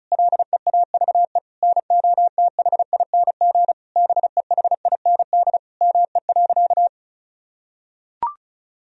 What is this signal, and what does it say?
Morse code, heard as a keyed tone, about 36 wpm, 700 Hz. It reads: LEAVE NOTHING BEHIND ME.
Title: Sentences from Top 1000 Words - Review Part 1 - 35wpm